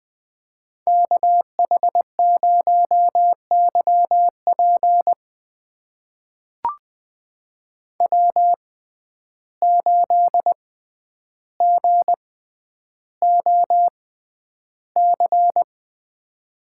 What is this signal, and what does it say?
Morse code, heard as a keyed tone, about 20 wpm, spaced as for 10 wpm, 700 Hz. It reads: KH0YP W8GOC
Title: USA Call Signs - 5 Characters - 6x Character Spacing - 20wpm